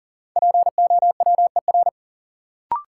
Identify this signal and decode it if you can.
Morse code, heard as a keyed tone, about 40 wpm, 700 Hz. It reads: POWER
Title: Sets of 4 Words - New Word Drill - 40wpm